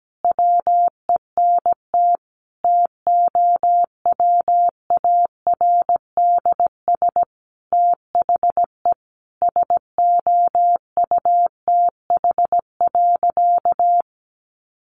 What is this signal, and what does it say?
Morse code, heard as a keyed tone, about 17 wpm, 700 Hz. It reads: WENT TOWARDS THE SOUTH.
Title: Sentences from Top 1100 Words 17wpm